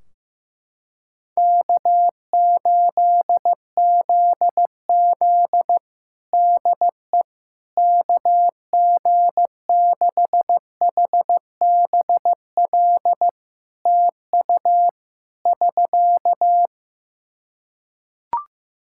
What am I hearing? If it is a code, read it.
Morse code, heard as a keyed tone, about 15 wpm, 700 Hz. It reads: K8ZZ DE KG6HBL TU <SK>
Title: Practice QSO - Exchange 4 - Ending - 15wpm